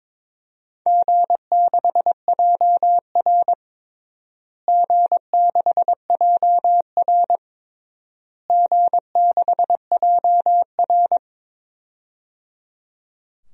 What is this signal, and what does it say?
Morse code, heard as a keyed tone, about 22 wpm, 700 Hz. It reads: G6JR G6JR G6JR
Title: DXCC-100 Call Signs - 4 Characters - Repeated 3x - 22wpm